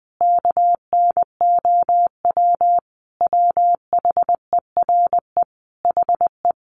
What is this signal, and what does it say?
Morse code, heard as a keyed tone, about 20 wpm, 700 Hz. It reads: KNOW WHERE HE
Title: Sentences from Top 600 Words 20wpm